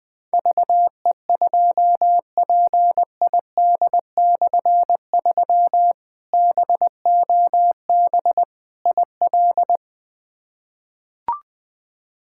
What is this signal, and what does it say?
Morse code, heard as a keyed tone, about 20 wpm, 700 Hz. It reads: VE2PID/3 BOB IL
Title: K1USN SST QSO Practice - 20wpm